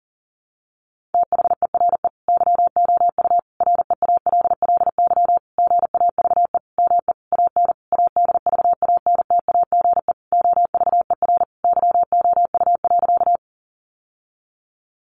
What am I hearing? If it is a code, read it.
Morse code, heard as a keyed tone, about 40 wpm, 700 Hz. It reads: THERE YOU REALLY GAVE ME AN ADVANTAGE OVER YOU.
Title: Sentences from Top 900 Words 40wpm